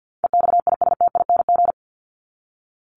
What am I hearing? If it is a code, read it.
Morse code, heard as a keyed tone, about 50 wpm, 700 Hz. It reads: EXISTING
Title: ING Suffix - Top 100 Words with Base Words 50wpm